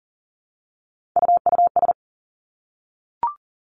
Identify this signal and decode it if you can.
Morse code, heard as a keyed tone, about 40 wpm, 700 Hz. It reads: UUS
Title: Three Challenging Letter-Number Combinations 40wpm